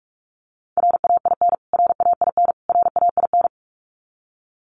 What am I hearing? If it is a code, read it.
Morse code, heard as a keyed tone, about 44 wpm, 700 Hz. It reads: RAIN RAIN RAIN
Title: Top 300 Words Repeated 45wpm